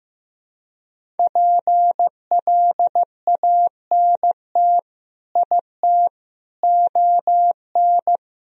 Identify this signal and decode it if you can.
Morse code, heard as a keyed tone, about 15 wpm, 700 Hz. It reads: PLANT IT ON
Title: Sentences from Top 1000 Words - Review Part 1 - 15wpm